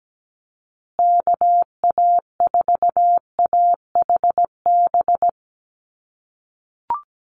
Morse code, heard as a keyed tone, about 17 wpm, 700 Hz. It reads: KA4AHB
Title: USA Call Signs Repeated 2x 17wpm